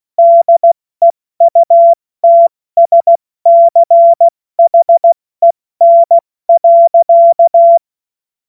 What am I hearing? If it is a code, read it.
Morse code, heard as a keyed tone, about 16 wpm, 680 Hz. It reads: DEUTSCHEN.
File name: Morsekurs / Witze / Witz 30. Dezember 2025 70 F80